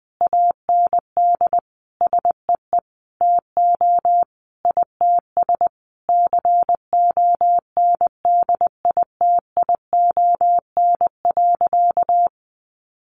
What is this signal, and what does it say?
Morse code, heard as a keyed tone, about 20 wpm, 700 Hz. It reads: AND SEE TO ITS CONDITION.
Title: Sentences from Top 1100 Words 20wpm